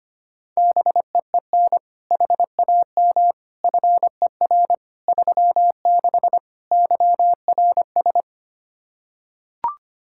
Morse code, heard as a keyed tone, about 25 wpm, 700 Hz. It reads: BEEN HAM FER 36 YRS
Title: Practice QSO - Exchange 3 - More Info - 25wpm